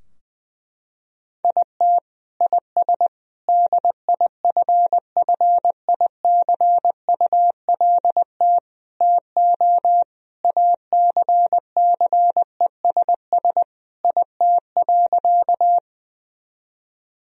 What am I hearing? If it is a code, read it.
Morse code, heard as a keyed tone, about 19 wpm, 700 Hz. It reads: IT IS DIFFICULT TO ACCESS IT.
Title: Sentences from Top 1000 Words - Review Part 4 - 20wpm